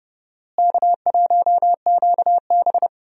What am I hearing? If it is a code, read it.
Morse code, heard as a keyed tone, about 30 wpm, 700 Hz. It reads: K1QB
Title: USA Call Signs - 4 Characters - Repeated 2x - 30wpm